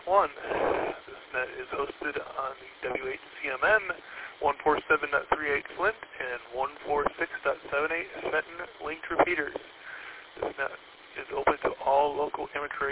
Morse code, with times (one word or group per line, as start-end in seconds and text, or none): none